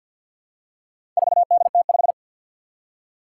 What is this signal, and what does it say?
Morse code, heard as a keyed tone, about 50 wpm, 700 Hz. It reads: 4DT5
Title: Four Challenging Letter-Number Combinations 50wpm